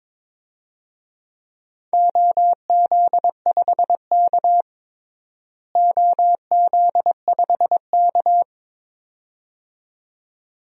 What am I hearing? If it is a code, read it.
Morse code, heard as a keyed tone, about 21 wpm, 700 Hz. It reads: OZ5K OZ5K
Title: DXCC-100 Call Signs - 4 Characters - Repeated 2x - 22wpm